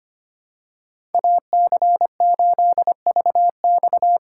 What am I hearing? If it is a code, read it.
Morse code, heard as a keyed tone, about 25 wpm, 700 Hz. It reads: AC8VX
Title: USA Call Signs - 5 Characters - 6x Character Spacing - 25wpm